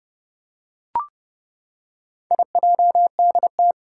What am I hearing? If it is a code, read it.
Morse code, heard as a keyed tone, about 30 wpm, 700 Hz. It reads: IJDT
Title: Four Challenging Letter-Number Combinations 30wpm